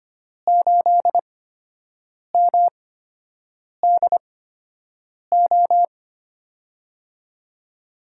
Morse code, heard as a keyed tone, about 25 wpm, spaced as for 7 wpm, 700 Hz. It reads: 8MDO
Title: USA Call Signs - 6 Characters - 8x Character Spacing - 25wpm